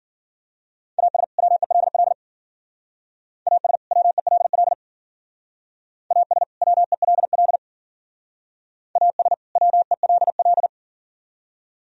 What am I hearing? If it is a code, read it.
Morse code, heard as a keyed tone, about 52 wpm, 700 Hz. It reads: AS WELL AS WELL AS WELL AS WELL